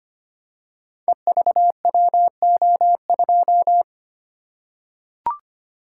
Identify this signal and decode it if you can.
Morse code, heard as a keyed tone, about 25 wpm, 700 Hz. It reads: EVWO2
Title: Code Groups - 5 Letter:Numbers - 25wpm